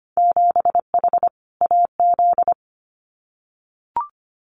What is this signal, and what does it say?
Morse code, heard as a keyed tone, about 25 wpm, 700 Hz. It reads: 7H AZ
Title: Winter Field Day Exchanges Encore 25wpm